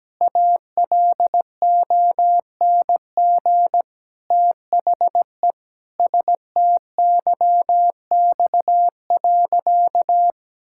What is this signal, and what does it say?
Morse code, heard as a keyed tone, about 17 wpm, 700 Hz. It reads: ALONG THE STYX.
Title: Pangram Challenge v2 - 17wpm